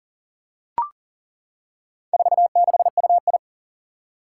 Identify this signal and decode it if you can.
Morse code, heard as a keyed tone, about 39 wpm, 700 Hz. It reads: 46UI
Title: Four Challenging Letter-Number Combinations 40wpm